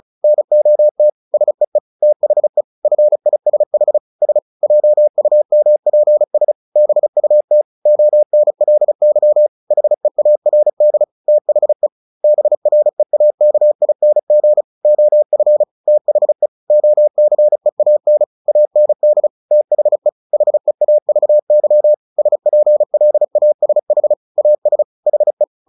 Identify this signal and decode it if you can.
Morse code, heard as a keyed tone, about 34 wpm, 600 Hz. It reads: NOT SEE THE FISH S JUMPS BUT ONLY HEARD THE BREAKING OF THE OCEAN AND THE HEAVY SPLASH AS HE
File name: the old man and the sea 2 35wpm